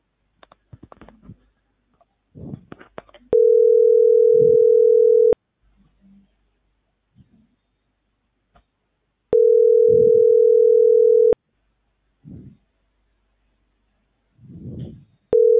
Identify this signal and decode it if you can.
Ringback tone (North America), 440 + 480 Hz.